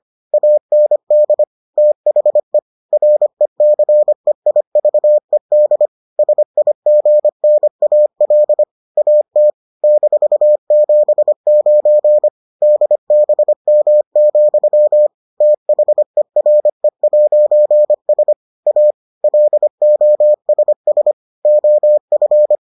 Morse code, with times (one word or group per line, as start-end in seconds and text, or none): 0.3-1.4 AND
1.8-2.6 THE
2.9-5.9 RECEIVED
6.2-8.6 SIGNAL
9.0-9.5 AT
9.8-12.3 -79
12.6-15.1 DBM,
15.4-18.3 THERE'S
18.7-18.9 A
19.2-21.1 LOSS
21.4-22.6 OF